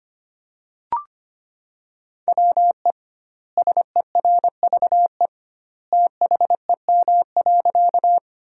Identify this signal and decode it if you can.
Morse code, heard as a keyed tone, about 25 wpm, 700 Hz. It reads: WE SERVE THEM.
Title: 2x Word-Spacing - Sentences from Top 400 Words - 25wpm